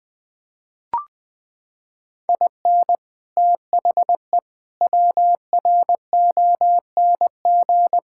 Morse code, heard as a keyed tone, about 21 wpm, 700 Hz. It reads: IN THE WRONG
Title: Sets of 8 Words - 20wpm